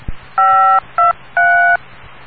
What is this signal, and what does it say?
DTMF, heard as telephone touch tones, tones of about 280 ms. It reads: 523